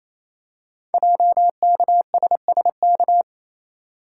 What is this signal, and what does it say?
Morse code, heard as a keyed tone, about 29 wpm, 700 Hz. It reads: JKSSK
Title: Code Groups - 5 Letter:Numbers - 28wpm